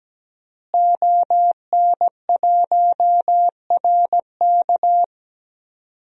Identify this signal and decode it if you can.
Morse code, heard as a keyed tone, about 17 wpm, 700 Hz. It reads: ON1RK
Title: DXCC-100 Call Signs - 5 Characters - Repeated 3x - 17wpm